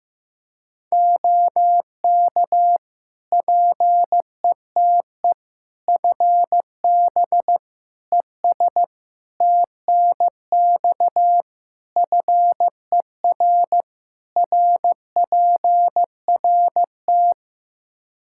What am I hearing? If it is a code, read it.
Morse code, heard as a keyed tone, about 15 wpm, 700 Hz. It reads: OK PETE FB ES TNX FER RPRT
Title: Practice QSO - Exchange 2 - Info - 15wpm